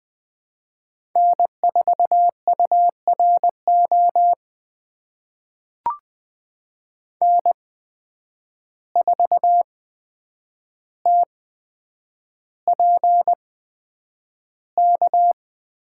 Morse code, heard as a keyed tone, about 20 wpm, spaced as for 9 wpm, 700 Hz. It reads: N4URO N4TPK
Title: USA Call Signs - 5 Characters - 8x Character Spacing - 20wpm